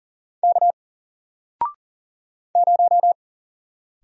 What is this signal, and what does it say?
Morse code, heard as a keyed tone, about 40 wpm, 700 Hz. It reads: K 0
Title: Single Letter-Number Alt 40wpm